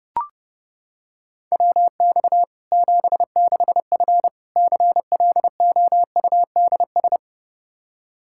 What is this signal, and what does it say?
Morse code, heard as a keyed tone, about 30 wpm, 700 Hz. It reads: WX 76F CLOUDS
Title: Practice QSO - Exchange 2 - Info - 30wpm